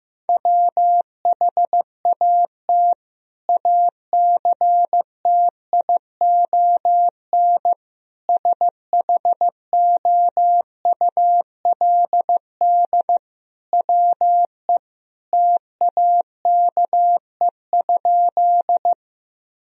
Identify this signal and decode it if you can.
Morse code, heard as a keyed tone, about 15 wpm, 700 Hz. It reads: WHAT ACTION SHOULD WE TAKE?
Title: Sentences from Top 1000 Words - Review Part 4 - 15wpm